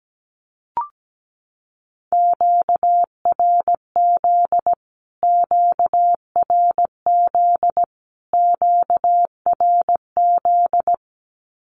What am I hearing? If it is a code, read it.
Morse code, heard as a keyed tone, about 17 wpm, 700 Hz. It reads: QRZ QRZ QRZ
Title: Q-Codes Repeated 3x 17wpm